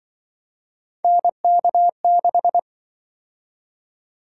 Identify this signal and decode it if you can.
Morse code, heard as a keyed tone, about 26 wpm, 700 Hz. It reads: NK6